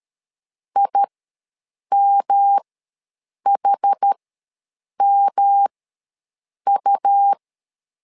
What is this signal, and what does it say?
Morse code, heard as a keyed tone, about 13 wpm, spaced as for 7 wpm, 800 Hz. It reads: IMHMU
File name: LEZIONI / 3 MWRO / 3 M ws 12wpm 1745